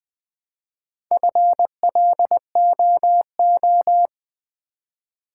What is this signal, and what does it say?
Morse code, heard as a keyed tone, about 21 wpm, 700 Hz. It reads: FLOO
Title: Build A Word - 1 Letter At A Time - 3 to 5-Letter Words - 20wpm